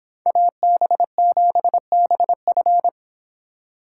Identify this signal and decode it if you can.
Morse code, heard as a keyed tone, about 25 wpm, 700 Hz. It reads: AB7BF